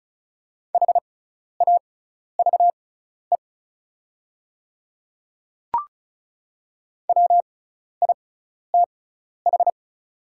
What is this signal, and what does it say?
Morse code, heard as a keyed tone, about 36 wpm, spaced as for 12 wpm, 700 Hz. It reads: HAVE WITH
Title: 6x Character Spacing - Top 100 Words - 35wpm